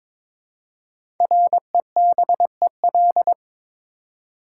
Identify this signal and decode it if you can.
Morse code, heard as a keyed tone, about 22 wpm, 700 Hz. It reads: REBEL
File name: 5 Letter Words - First 3 Letters Given - 22wpm